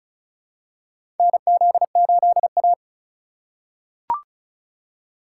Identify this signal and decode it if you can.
Morse code, heard as a keyed tone, about 34 wpm, 700 Hz. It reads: NZ8A
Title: USA Call Signs - 4 Characters - 2x Character Spacing - 35wpm